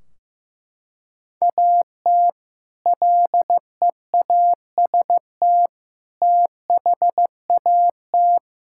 Morse code, heard as a keyed tone, about 15 wpm, 700 Hz. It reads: AT LEAST THAT
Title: Sentences from Top 1600 Words 15wpm